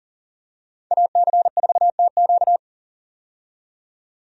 Morse code, heard as a keyed tone, about 40 wpm, 700 Hz. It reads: AC4TQ
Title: USA Call Signs - 5 Characters - 40wpm